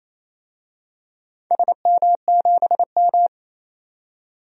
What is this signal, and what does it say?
Morse code, heard as a keyed tone, about 28 wpm, 700 Hz. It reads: SM7M
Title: DXCC-100 Call Signs - 4 Characters - 28wpm